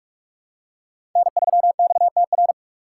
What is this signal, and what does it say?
Morse code, heard as a keyed tone, about 44 wpm, 690 Hz. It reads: N3XTR
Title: USA Call Signs - 5 Characters - 6x Character Spacing - 45wpm